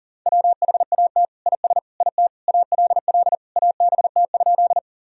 Morse code, heard as a keyed tone, about 40 wpm, 700 Hz. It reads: WHAT IS IT ALL ABT?
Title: Sentences from Top 100 Words and Abbreviations 40wpm